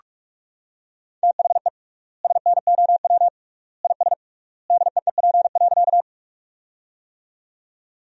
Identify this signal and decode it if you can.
Morse code, heard as a keyed tone, about 46 wpm, 700 Hz. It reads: THE SNOW IS DEEP.
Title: 3x Word-Spacing - Sentences from Top 500 Words - 45wpm